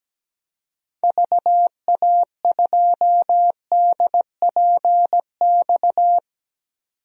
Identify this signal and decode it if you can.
Morse code, heard as a keyed tone, about 17 wpm, 700 Hz. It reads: VA2DPX
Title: DXCC-100 Call Signs - 6 Characters - 17wpm